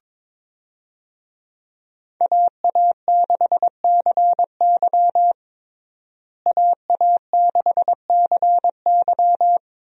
Morse code, heard as a keyed tone, about 22 wpm, 700 Hz. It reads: AA6CY AA6CY
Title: USA Call Signs - 5 Characters - Repeated 3x - 22wpm